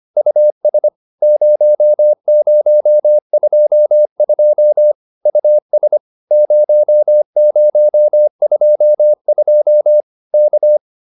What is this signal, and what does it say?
Morse code, heard as a keyed tone, about 25 wpm, 590 Hz. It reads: US 0022 US 0022 K
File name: POTA205a 25